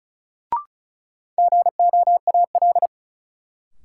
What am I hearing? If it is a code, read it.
Morse code, heard as a keyed tone, about 36 wpm, 700 Hz. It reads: GOAL
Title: Sentences from Top 1400 Words - New Word Drill Alt - 35wpm